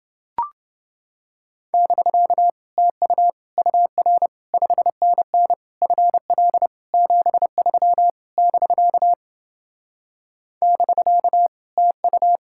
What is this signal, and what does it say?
Morse code, heard as a keyed tone, about 29 wpm, 700 Hz. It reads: <BK> TU UR 5NN FL 73 <BK> <BK> TU